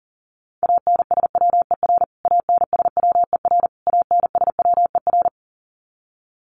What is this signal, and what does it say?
Morse code, heard as a keyed tone, about 40 wpm, 700 Hz. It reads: ANSWER ANSWER ANSWER
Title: Top 300 Words Repeated 40wpm